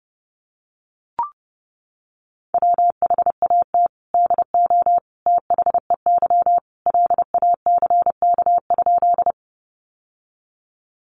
Morse code, heard as a keyed tone, about 30 wpm, 700 Hz. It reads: WHAT DO THEY LACK?
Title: Sentences from Top 1100 Words 30wpm